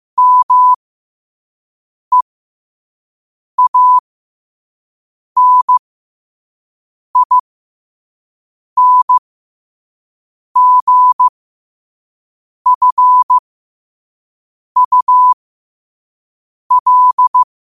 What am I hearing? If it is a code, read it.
Morse code, heard as a keyed tone, about 15 wpm, spaced as for 5 wpm, 1000 Hz. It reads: MEANINGFUL
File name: qotdfiles / 5wpm